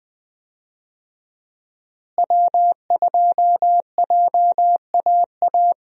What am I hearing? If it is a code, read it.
Morse code, heard as a keyed tone, about 21 wpm, 700 Hz. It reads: W2JAA
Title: USA Call Signs - 5 Characters - Repeated 2x - 20wpm